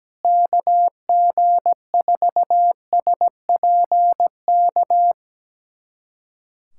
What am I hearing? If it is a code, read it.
Morse code, heard as a keyed tone, about 17 wpm, 700 Hz. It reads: KG4SPK
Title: USA Callsign DE Callsign 17wpm